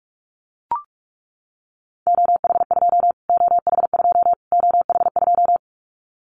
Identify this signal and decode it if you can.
Morse code, heard as a keyed tone, about 44 wpm, 700 Hz. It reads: OH2 OH2 OH2
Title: Callsign Prefixes for Top 25 DXCC - Repeated 3x - 45wpm